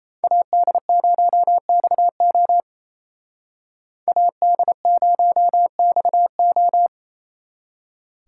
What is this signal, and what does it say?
Morse code, heard as a keyed tone, about 36 wpm, 700 Hz. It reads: AD0XO AD0XO